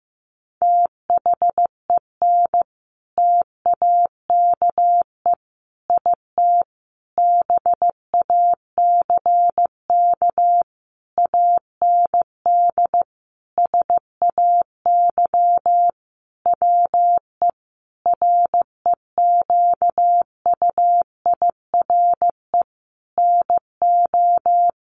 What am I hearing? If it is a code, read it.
Morse code, heard as a keyed tone, about 15 wpm, 700 Hz. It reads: THEN TAKE IT BACK AND SAY WE REQUIRE NO
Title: Sentences from Top 1500 Words 15wpm